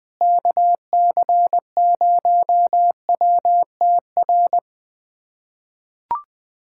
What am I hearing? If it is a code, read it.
Morse code, heard as a keyed tone, about 20 wpm, 700 Hz. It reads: KC0WTR
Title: USA Callsign DE Callsign 20wpm